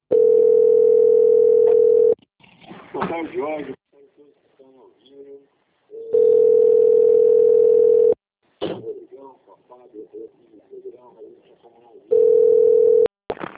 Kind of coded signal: telephone call progress tone

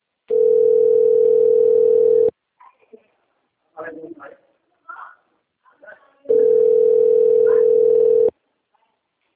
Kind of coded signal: telephone call progress tone

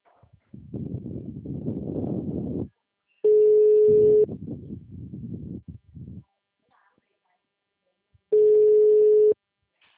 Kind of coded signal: telephone call progress tone

